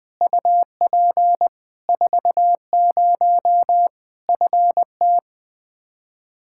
Morse code, heard as a keyed tone, about 20 wpm, 700 Hz. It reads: UP 40 FT
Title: Practice QSO - Exchange 2 - Info - 20wpm